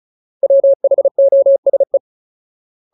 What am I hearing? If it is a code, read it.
Morse code, heard as a keyed tone, about 34 wpm, 550 Hz. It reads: WHOSE
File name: Ultimate Word List / 35WPM Ultimate